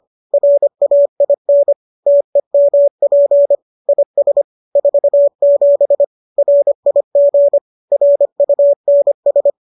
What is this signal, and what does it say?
Morse code, heard as a keyed tone, about 25 wpm, 570 Hz. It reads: RAIN TEMP IS 47 RIG RUNS
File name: qso206 25